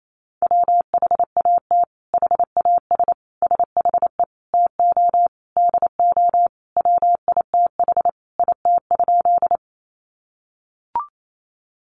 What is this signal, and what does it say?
Morse code, heard as a keyed tone, about 28 wpm, 700 Hz. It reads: WHAT HAS SHE TO DO WITH IT?
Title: Sentences from Top 100 Words 28wpm